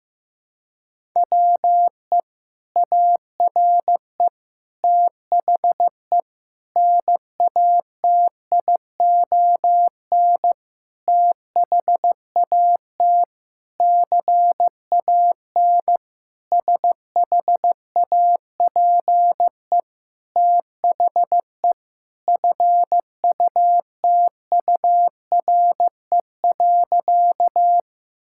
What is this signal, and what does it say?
Morse code, heard as a keyed tone, about 15 wpm, 700 Hz. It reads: WE ARE THE NATION THAT CAN SHAPE THE FUTURE.
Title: Sentences from Top 1000 Words - Review Part 6 - 15wpm